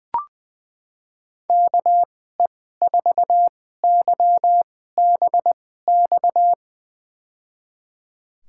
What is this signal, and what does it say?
Morse code, heard as a keyed tone, about 20 wpm, spaced as for 14 wpm, 700 Hz. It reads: KE4YBX
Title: USA Call Signs - 6 Characters - 2x Character Spacing - 20wpm